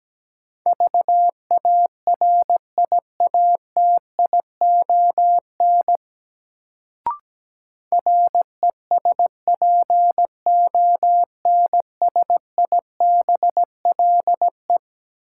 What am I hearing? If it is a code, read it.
Morse code, heard as a keyed tone, about 17 wpm, 700 Hz. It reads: VARIATION RESPONSIBLE
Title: Sentences from Top 1600 Words - New Word Drill Alt - 17wpm